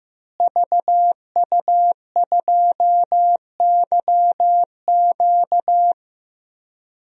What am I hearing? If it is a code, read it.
Morse code, heard as a keyed tone, about 15 wpm, 700 Hz. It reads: VU2YQ
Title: DXCC-100 Call Signs - 5 Characters - Repeated 3x - 15wpm